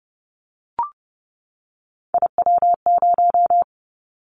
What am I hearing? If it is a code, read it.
Morse code, heard as a keyed tone, about 31 wpm, 700 Hz. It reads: IW0
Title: Callsign Prefixes for Top 50 DXCC - 30wpm